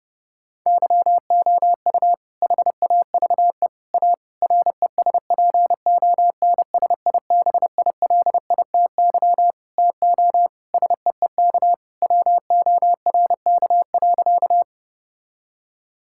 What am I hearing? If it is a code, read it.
Morse code, heard as a keyed tone, about 32 wpm, 700 Hz. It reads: YOU HAVE A RESPONSIBILITY TO SEEK WORK.
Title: Sentences from Top 1600 Words 30wpm